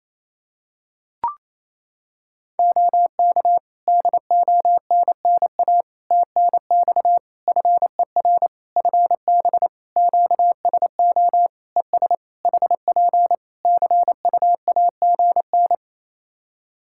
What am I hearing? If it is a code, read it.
Morse code, heard as a keyed tone, about 28 wpm, 700 Hz. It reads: OK DONNA TNX FER FB QSO ES HP CUAGN
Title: Practice QSO - Exchange 4 - Ending - 28wpm